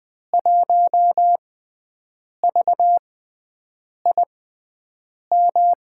Morse code, heard as a keyed tone, about 20 wpm, spaced as for 7 wpm, 700 Hz. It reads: 1VIM
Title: USA Call Signs - 5 Characters - 6x Character Spacing - 20wpm